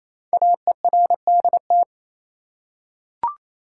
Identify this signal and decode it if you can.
Morse code, heard as a keyed tone, about 27 wpm, 700 Hz. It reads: AERDT
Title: Code Groups - 5 Letters - 28wpm